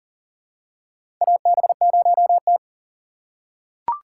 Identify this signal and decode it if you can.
Morse code, heard as a keyed tone, about 40 wpm, 700 Hz. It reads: AB0T
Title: USA Call Signs - 4 Characters - 6x Character Spacing - 40wpm